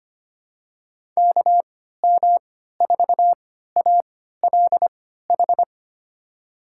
Morse code, heard as a keyed tone, about 25 wpm, spaced as for 14 wpm, 700 Hz. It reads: KM4ALH
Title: USA Call Signs - 6 Characters - 3x Character Spacing - 25wpm